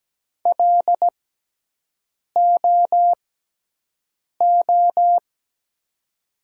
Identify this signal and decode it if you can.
Morse code, heard as a keyed tone, about 17 wpm, spaced as for 6 wpm, 700 Hz. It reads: LOO
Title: 6x Character Spacing - Top 100 Words - 17wpm